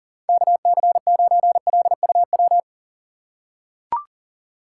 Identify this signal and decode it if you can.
Morse code, heard as a keyed tone, about 40 wpm, 700 Hz. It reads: KC9LUW
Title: USA Call Signs - 6 Characters - 2x Character Spacing - 40wpm